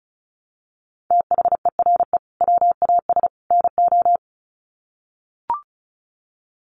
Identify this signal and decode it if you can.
Morse code, heard as a keyed tone, about 34 wpm, 700 Hz. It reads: THERE WAS NO
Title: Sets of 3 Words 35wpm